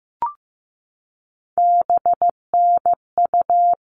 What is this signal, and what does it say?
Morse code, heard as a keyed tone, about 15 wpm, 700 Hz. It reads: BNU
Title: All Cut Numbers - 3-Digit 15wpm